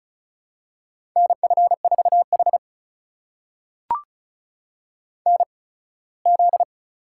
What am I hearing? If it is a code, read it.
Morse code, heard as a keyed tone, about 36 wpm, spaced as for 21 wpm, 700 Hz. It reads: NF4H NZ
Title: USA Call Signs - 4 Characters - 8x Character Spacing - 35wpm